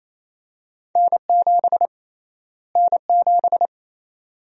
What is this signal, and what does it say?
Morse code, heard as a keyed tone, about 28 wpm, 700 Hz. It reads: N7 N7
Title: USA Call Sign Prefixes - Repeated 2x - 28wpm